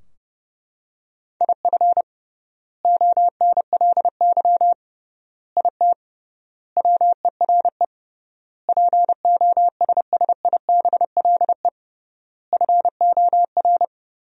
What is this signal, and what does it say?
Morse code, heard as a keyed tone, about 30 wpm, 700 Hz. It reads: IF ONLY IT WERE POSSIBLE FOR
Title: 3x Word-Spacing - Sentences from Top 500 Words - 30wpm